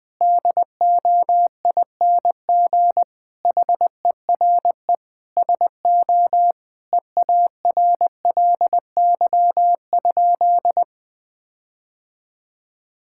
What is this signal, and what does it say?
Morse code, heard as a keyed tone, about 20 wpm, 700 Hz. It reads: DOING HERE SO EARLY?
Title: Sentences with 5-Word Phrases - 20wpm